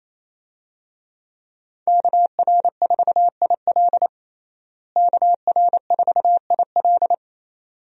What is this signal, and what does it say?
Morse code, heard as a keyed tone, about 28 wpm, 700 Hz. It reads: KR4IL KR4IL
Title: USA Call Signs - 5 Characters - Repeated 2x - 28wpm